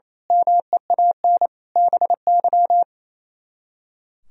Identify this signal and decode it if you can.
Morse code, heard as a keyed tone, about 30 wpm, 700 Hz. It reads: MEAN BY
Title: Speed-Racing - Sets of 5 Words 28wpm